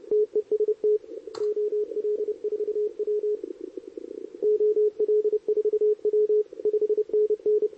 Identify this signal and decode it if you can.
Morse code, heard as a keyed tone, about 31 wpm, 440 Hz, with radio noise. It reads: TEST OL4W OL4W 5NN
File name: CQWWCW2023 / SP